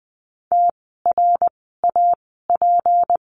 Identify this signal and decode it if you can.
Morse code, heard as a keyed tone, about 20 wpm, spaced as for 15 wpm, 700 Hz. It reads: TRAP
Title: 153 Lesson 25f - Letter p - Words 20wpm